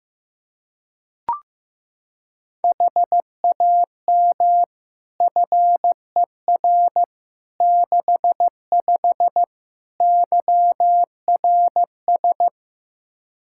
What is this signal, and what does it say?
Morse code, heard as a keyed tone, about 15 wpm, 700 Hz. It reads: HAM FER 65 YRS
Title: Practice QSO - Exchange 3 - More Info - 15wpm